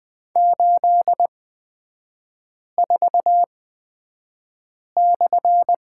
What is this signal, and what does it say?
Morse code, heard as a keyed tone, about 20 wpm, spaced as for 5 wpm, 700 Hz. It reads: 84/
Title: 255 Lesson 41e - Period - Three Characters 20wpm